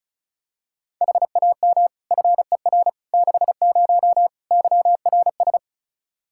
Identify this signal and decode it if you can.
Morse code, heard as a keyed tone, about 36 wpm, 700 Hz. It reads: HAM FER 60 YRS
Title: Practice QSO - Exchange 3 - More Info - 35wpm